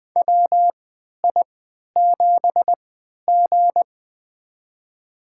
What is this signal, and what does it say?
Morse code, heard as a keyed tone, about 21 wpm, spaced as for 12 wpm, 700 Hz. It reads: WI7G
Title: USA Call Signs - 4 Characters - 3x Character Spacing - 20wpm